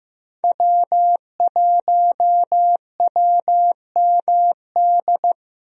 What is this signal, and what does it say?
Morse code, heard as a keyed tone, about 16 wpm, 700 Hz. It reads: W1WMD
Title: USA Call Signs - 5 Characters - Repeated 3x - 15wpm